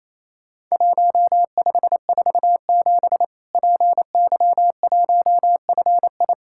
Morse code, heard as a keyed tone, about 28 wpm, 700 Hz. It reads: 1547 PY1FI
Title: Worked All Europe DX Contest - QTC Exchange - 28wpm